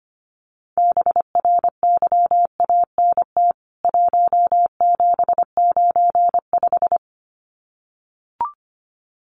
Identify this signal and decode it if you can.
Morse code, heard as a keyed tone, about 25 wpm, 700 Hz. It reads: BRYANT 1795
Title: CWT Exchange 25wpm